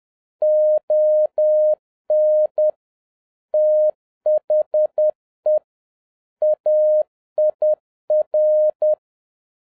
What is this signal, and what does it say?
Morse code, heard as a keyed tone, about 10 wpm, 610 Hz. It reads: ON THE AIR
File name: 2011-07-mp3 / n21 GREETINGS FROM SINGAPORE 10wpm